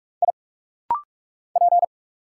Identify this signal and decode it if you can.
Morse code, heard as a keyed tone, about 44 wpm, 700 Hz. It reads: I P
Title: Single Letter-Number Alt - Rapid-Fire 45wpm